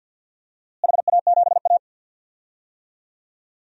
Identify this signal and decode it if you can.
Morse code, heard as a keyed tone, about 53 wpm, 700 Hz. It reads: HA7A
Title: DXCC-100 Call Signs - 4 Characters - Repeated 3x - 50wpm